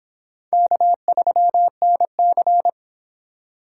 Morse code, heard as a keyed tone, about 25 wpm, 700 Hz. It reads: K3NC